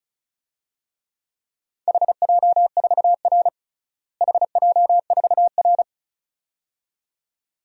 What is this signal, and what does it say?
Morse code, heard as a keyed tone, about 36 wpm, 700 Hz. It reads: HJ4R HJ4R
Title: DXCC-100 Call Signs - 4 Characters - Repeated 2x - 35wpm